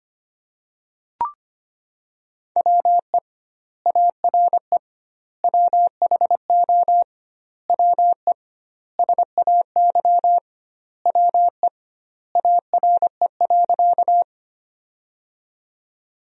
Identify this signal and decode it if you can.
Morse code, heard as a keyed tone, about 25 wpm, 700 Hz. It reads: WE ARE WHO WE SAY WE ARE.
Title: 2x Word-Spacing - Sentences from Top 200 Words - 25wpm